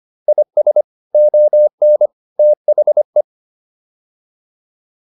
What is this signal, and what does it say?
Morse code, heard as a keyed tone, about 25 wpm, 600 Hz. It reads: IS ON THE